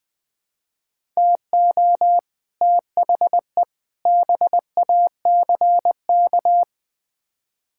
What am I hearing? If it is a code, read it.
Morse code, heard as a keyed tone, about 20 wpm, 700 Hz. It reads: TO THE BACK
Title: Build A Phrase - 5-Word Phrases - 20wpm